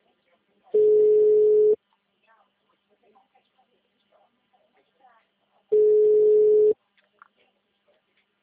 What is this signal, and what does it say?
Ringback tone (Europe, 425 Hz), with speech.